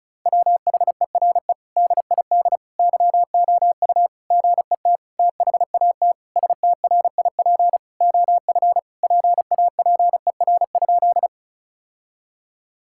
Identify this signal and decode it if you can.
Morse code, heard as a keyed tone, about 34 wpm, 700 Hz. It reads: WHERE DID YOU GET THAT STRIP OF PAPER?
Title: Sentences from Top 1600 Words 35wpm